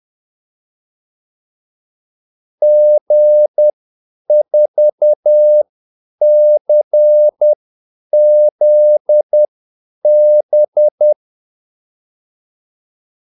Morse code, heard as a keyed tone, about 10 wpm, spaced as for 8 wpm, 600 Hz. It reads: G4CZB
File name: Callsigns-10-8wpm